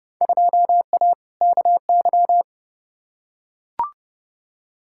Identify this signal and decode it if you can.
Morse code, heard as a keyed tone, about 30 wpm, 700 Hz. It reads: 2A KY
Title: Field Day Exchanges - 30wpm